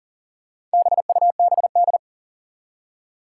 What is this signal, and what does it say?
Morse code, heard as a keyed tone, about 43 wpm, 700 Hz. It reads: BUBD
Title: Four Challenging Letter-Number Combinations 40wpm